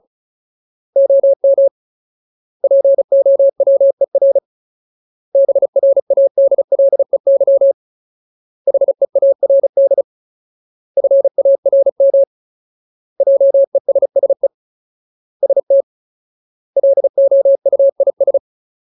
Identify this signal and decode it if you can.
Morse code, heard as a keyed tone, about 34 wpm, 550 Hz. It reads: OM POWER BRADLEY HEARD FARM JESSE ST LOUIS
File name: Ultimate Word List / 35WPM Ultimate